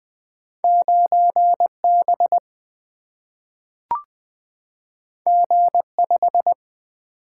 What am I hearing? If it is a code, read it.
Morse code, heard as a keyed tone, about 20 wpm, 700 Hz. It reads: 9 B G 5
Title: Single-Letter Single-Number 20wpm